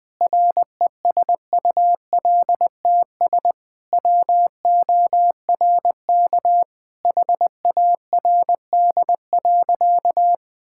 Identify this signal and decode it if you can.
Morse code, heard as a keyed tone, about 21 wpm, 700 Hz. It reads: RESULTS WORK HARD.
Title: Sentences with 3-Word Phrases - 20wpm